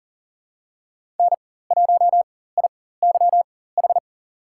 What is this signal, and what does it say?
Morse code, heard as a keyed tone, about 40 wpm, spaced as for 19 wpm, 700 Hz. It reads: N1IYH
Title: USA Call Signs - 5 Characters - 4x Character Spacing - 40wpm